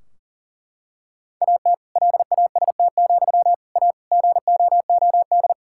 Morse code, heard as a keyed tone, about 40 wpm, 700 Hz. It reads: AT LAST, A GOOD
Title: Sentences from Top 1600 Words 40wpm